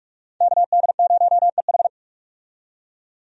Morse code, heard as a keyed tone, about 47 wpm, 700 Hz. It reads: KD0EH
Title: USA Call Signs - 5 Characters - 45wpm